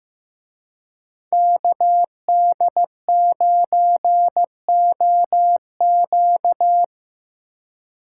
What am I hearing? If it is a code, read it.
Morse code, heard as a keyed tone, about 18 wpm, 700 Hz. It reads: KD9OQ
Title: Speed Racing - USA 2x2 Call Signs 15wpm